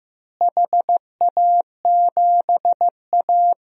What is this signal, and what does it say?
Morse code, heard as a keyed tone, about 15 wpm, 700 Hz. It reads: HA7A
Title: DXCC-100 Call Signs - 4 Characters - Repeated 3x - 15wpm